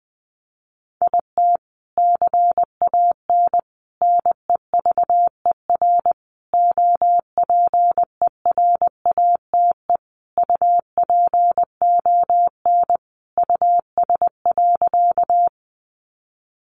Morse code, heard as a keyed tone, about 20 wpm, 700 Hz. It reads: IT CAN NEVER OPERATE UPON US.
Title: Sentences from Top 1500 Words 20wpm